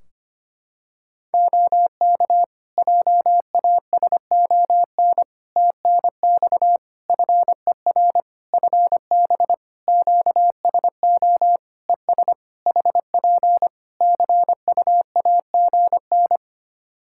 Morse code, heard as a keyed tone, about 25 wpm, 700 Hz. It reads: OK JASON TNX FER FB QSO ES HP CUAGN
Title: Practice QSO - Exchange 4 - Ending - 25wpm